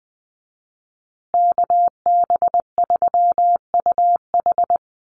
Morse code, heard as a keyed tone, about 20 wpm, 700 Hz. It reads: KB3UH
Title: USA Call Signs Repeated 3x 20wpm